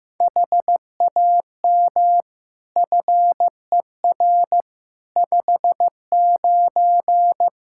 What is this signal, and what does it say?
Morse code, heard as a keyed tone, about 15 wpm, 700 Hz. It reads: HAM FER 59
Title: Practice QSO - Exchange 3 - More Info - 15wpm